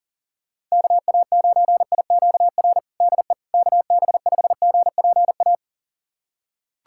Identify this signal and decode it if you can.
Morse code, heard as a keyed tone, about 40 wpm, 700 Hz. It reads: KA9IQR DE KB5GPA
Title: USA Callsign DE Callsign - 6 Characters - 40wpm